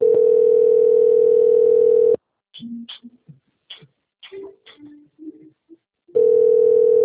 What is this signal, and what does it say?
Ringback tone (North America), 440 + 480 Hz.
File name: out- - - - - .16901